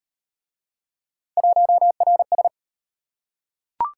Morse code, heard as a keyed tone, about 34 wpm, 700 Hz. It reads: 1RS